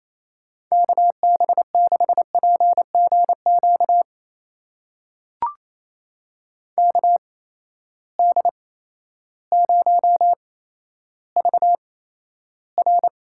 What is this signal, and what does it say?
Morse code, heard as a keyed tone, about 27 wpm, spaced as for 13 wpm, 700 Hz. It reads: KB6PGQ KD0VR
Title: USA Call Signs - 6 Characters - 8x Character Spacing - 28wpm